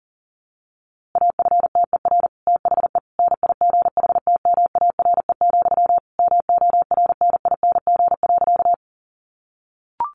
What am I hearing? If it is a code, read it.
Morse code, heard as a keyed tone, about 40 wpm, 700 Hz. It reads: AFTER THE NIGHTMARE, MORNING.
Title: A Sound Of Thunder 40wpm